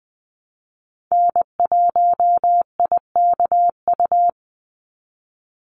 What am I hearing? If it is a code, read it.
Morse code, heard as a keyed tone, about 20 wpm, 700 Hz. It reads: N1IKU
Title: USA Call Signs 20wpm